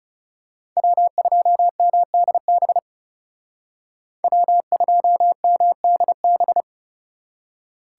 Morse code, heard as a keyed tone, about 34 wpm, 700 Hz. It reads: W2MDB W2MDB
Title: Speed Racing - USA 1x3 Call Signs 30wpm